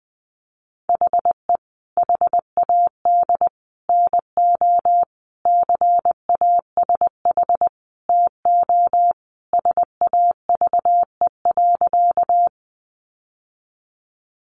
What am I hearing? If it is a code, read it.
Morse code, heard as a keyed tone, about 20 wpm, 700 Hz. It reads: HE HAD NO CASH TO SAVE.
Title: Sentences from Top 1100 Words 20wpm